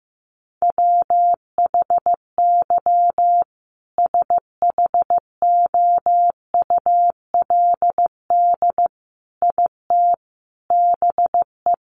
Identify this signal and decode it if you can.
Morse code, heard as a keyed tone, about 15 wpm, 700 Hz. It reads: WHY SHOULD IT BE
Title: Sentences from Top 1200 Words 15wpm